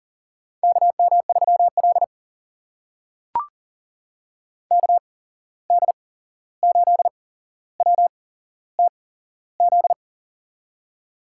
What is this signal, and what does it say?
Morse code, heard as a keyed tone, about 40 wpm, spaced as for 15 wpm, 700 Hz. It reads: KM3L KD8WTZ
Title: USA Call Signs - 8x Character Spacing - 40wpm